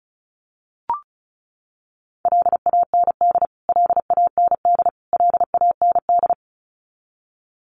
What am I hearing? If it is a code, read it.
Morse code, heard as a keyed tone, about 34 wpm, 700 Hz. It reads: LAND LAND LAND
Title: Top 300 Words Repeated 35wpm